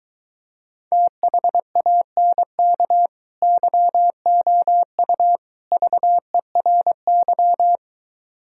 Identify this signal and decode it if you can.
Morse code, heard as a keyed tone, about 23 wpm, 700 Hz. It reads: THANK YOU VERY